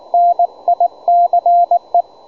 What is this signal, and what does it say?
Morse code, heard as a keyed tone, about 19 wpm, 700 Hz, with radio noise. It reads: NICE
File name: juma-trx2 / juma-trx2-30m-cw-sample